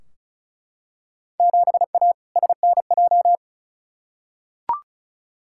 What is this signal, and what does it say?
Morse code, heard as a keyed tone, about 36 wpm, 700 Hz. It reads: 7A SNJ
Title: Field Day Exchanges Encore - 35wpm